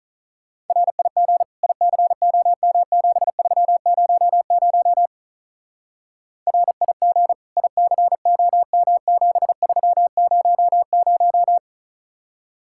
Morse code, heard as a keyed tone, about 46 wpm, 700 Hz. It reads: RIG ICOM7300 RIG ICOM7300